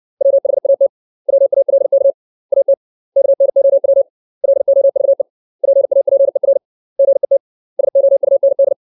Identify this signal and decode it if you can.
Morse code, heard as a keyed tone, about 60 wpm, 550 Hz. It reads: WHAT WALK AT KNOW LOVE PAPER GET SOUND
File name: 632 words / 632 Words 60WPM